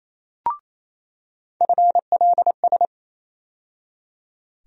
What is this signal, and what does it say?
Morse code, heard as a keyed tone, about 28 wpm, 700 Hz. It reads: FLS
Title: USA Call Sign Postfixes - 3 Letters 28wpm